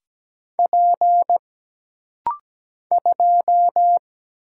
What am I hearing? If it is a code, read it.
Morse code, heard as a keyed tone, about 17 wpm, 700 Hz. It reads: P 2
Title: Single Letter-Number - Rapid-Fire 17wpm